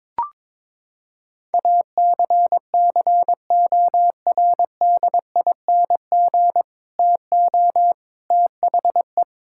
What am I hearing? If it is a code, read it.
Morse code, heard as a keyed tone, about 22 wpm, 700 Hz. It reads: ACCORDING TO THE
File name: practice / 22wpm / Sentences with 2-Word Phrases - 22wpm